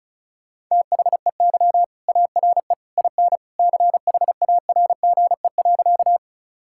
Morse code, heard as a keyed tone, about 34 wpm, 700 Hz. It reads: THEY ARE IN CHARGE.
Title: Sentences from Top 1000 Words - Review Part 6 - 35wpm